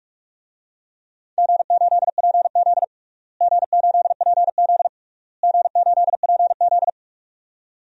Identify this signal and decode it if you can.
Morse code, heard as a keyed tone, about 47 wpm, 700 Hz. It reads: G8PZ G8PZ G8PZ
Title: DXCC-100 Call Signs - 4 Characters - Repeated 3x - 45wpm